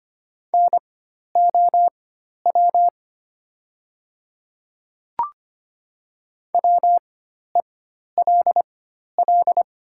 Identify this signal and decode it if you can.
Morse code, heard as a keyed tone, about 25 wpm, spaced as for 12 wpm, 690 Hz. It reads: NOW WELL
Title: 4x Character Spacing - Top 300 Words - 25wpm